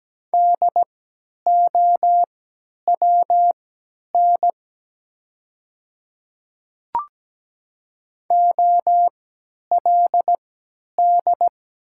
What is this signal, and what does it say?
Morse code, heard as a keyed tone, about 17 wpm, spaced as for 10 wpm, 700 Hz. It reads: DOWN OLD
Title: 3x Character Spacing - Top 300 Words - 17wpm